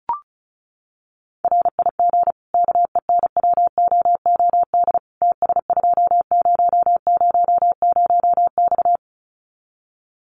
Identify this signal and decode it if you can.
Morse code, heard as a keyed tone, about 34 wpm, 700 Hz. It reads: RIG KENWOOD TS2000X
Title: Common QSO Phrases 35wpm